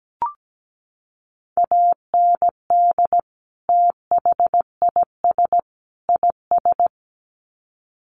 Morse code, heard as a keyed tone, about 17 wpm, 700 Hz. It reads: AND THIS IS
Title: Sets of 3 Words 17wpm